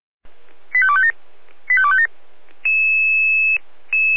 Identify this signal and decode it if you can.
Five-tone selective call (ZVEI), heard as quick runs of five tones, tones of about 70 ms. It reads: 85257 85257